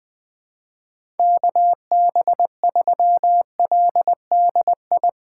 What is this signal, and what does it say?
Morse code, heard as a keyed tone, about 21 wpm, 700 Hz. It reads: KB3LDI
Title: USA Call Signs - 8x Character Spacing - 20wpm